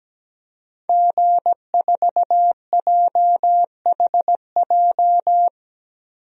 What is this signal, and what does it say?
Morse code, heard as a keyed tone, about 17 wpm, 700 Hz. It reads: G4JHJ
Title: DXCC-100 Call Signs - 5 Characters - Repeated 3x - 17wpm